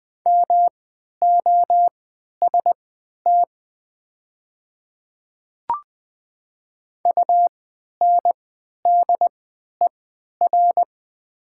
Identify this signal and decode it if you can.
Morse code, heard as a keyed tone, about 20 wpm, spaced as for 11 wpm, 700 Hz. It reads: MOST UNDER
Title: 3x Character Spacing - Top 200 Words - 20wpm